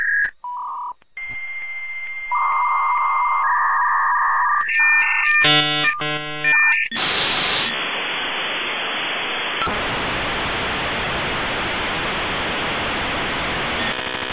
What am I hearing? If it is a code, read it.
Dial-up modem handshake.